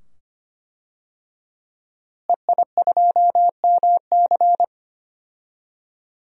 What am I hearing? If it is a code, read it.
Morse code, heard as a keyed tone, about 25 wpm, 700 Hz. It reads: EI2MC
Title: DXCC-100 Call Signs - 5 Characters - Repeated 2x - 25wpm